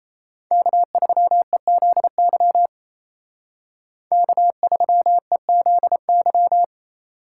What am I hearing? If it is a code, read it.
Morse code, heard as a keyed tone, about 35 wpm, 700 Hz. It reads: K3EZY K3EZY